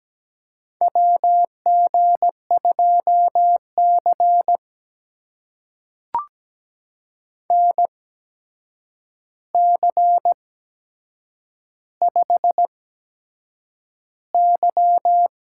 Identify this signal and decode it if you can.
Morse code, heard as a keyed tone, about 17 wpm, spaced as for 7 wpm, 700 Hz. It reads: WG2C NC5Y
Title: USA Call Signs - 4 Characters - 8x Character Spacing - 17wpm